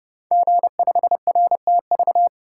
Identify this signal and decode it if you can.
Morse code, heard as a keyed tone, about 30 wpm, 700 Hz. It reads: G5RTV
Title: Code Groups - 5 Letter:Numbers - 30wpm